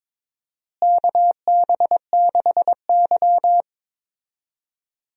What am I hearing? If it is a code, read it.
Morse code, heard as a keyed tone, about 21 wpm, 700 Hz. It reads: KB6Y
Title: USA Call Signs - 4 Characters - Repeated 3x - 22wpm